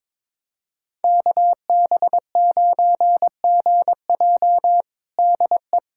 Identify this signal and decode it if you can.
Morse code, heard as a keyed tone, about 22 wpm, 700 Hz. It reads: KB9GJ DE
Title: USA Callsign DE Callsign - 5 Characters - 22wpm